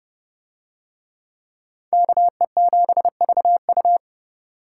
Morse code, heard as a keyed tone, about 29 wpm, 700 Hz. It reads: KE7VU